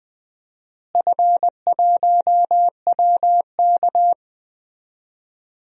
Morse code, heard as a keyed tone, about 20 wpm, 700 Hz. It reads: F1WK
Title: DXCC-100 Call Signs - 4 Characters - Repeated 3x - 20wpm